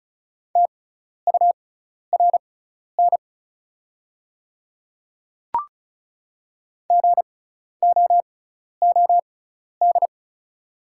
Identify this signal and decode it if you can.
Morse code, heard as a keyed tone, about 34 wpm, spaced as for 12 wpm, 690 Hz. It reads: TURN GOOD
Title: 6x Character Spacing - Top 200 Words - 35wpm